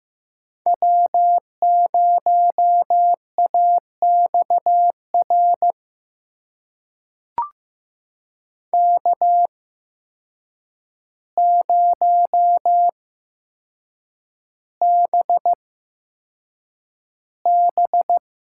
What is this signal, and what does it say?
Morse code, heard as a keyed tone, about 15 wpm, spaced as for 7 wpm, 700 Hz. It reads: W0AXR K0BB
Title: USA Call Signs - 5 Characters - 8x Character Spacing - 15wpm